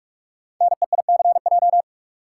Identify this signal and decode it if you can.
Morse code, heard as a keyed tone, about 44 wpm, 700 Hz. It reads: NEICJ
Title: Code Groups - 5 Letters - 45wpm